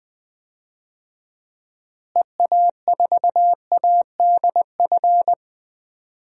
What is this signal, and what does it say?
Morse code, heard as a keyed tone, about 20 wpm, 700 Hz. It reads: EA4ADF
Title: DXCC-100 Call Signs - 6 Characters - Repeated 2x - 20wpm